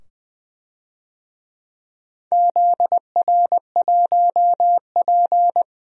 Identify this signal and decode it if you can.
Morse code, heard as a keyed tone, about 20 wpm, 700 Hz. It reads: ZR1P
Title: DXCC-100 Call Signs - 4 Characters - 20wpm